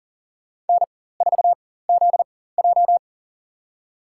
Morse code, heard as a keyed tone, about 37 wpm, spaced as for 18 wpm, 700 Hz. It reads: N4ZJ
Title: USA Call Signs - 4x Character Spacing - 40wpm